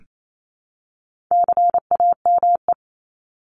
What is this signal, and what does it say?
Morse code, heard as a keyed tone, about 28 wpm, 700 Hz. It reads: CAME
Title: Top 200 Words Repeated 28wpm